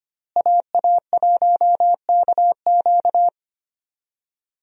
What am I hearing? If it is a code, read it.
Morse code, heard as a keyed tone, about 25 wpm, 700 Hz. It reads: AA1KQ
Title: USA Call Signs - 5 Characters - Repeated 3x - 25wpm